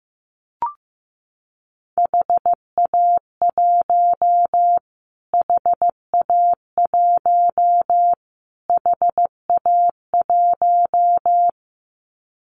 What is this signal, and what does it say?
Morse code, heard as a keyed tone, about 15 wpm, 700 Hz. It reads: HA1 HA1 HA1
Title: Callsign Prefixes for Top 25 DXCC - Repeated 3x - 15wpm